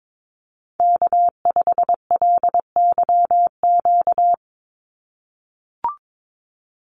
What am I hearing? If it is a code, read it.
Morse code, heard as a keyed tone, about 22 wpm, 700 Hz. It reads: K5LYQ
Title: USA Call Signs 22wpm